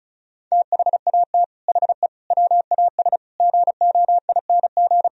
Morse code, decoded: THAT HE WAS GOING

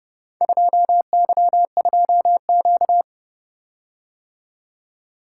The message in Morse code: 2Y2Q

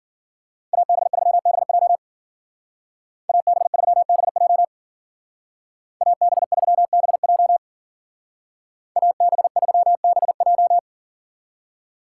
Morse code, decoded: AB3BJ AB3BJ AB3BJ AB3BJ

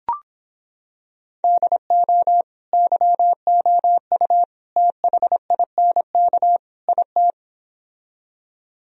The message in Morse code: DO YOU THINK IT

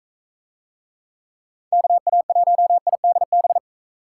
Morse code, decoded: KA1IDB

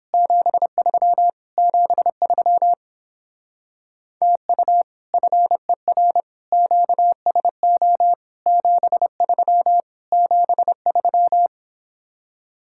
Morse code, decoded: 73 73 TU FER QSO 73 73

